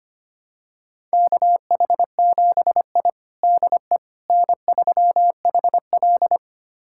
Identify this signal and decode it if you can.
Morse code, heard as a keyed tone, about 25 wpm, 700 Hz. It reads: KH7I DE N3HL
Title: USA Callsign DE Callsign - 4 Characters - 25wpm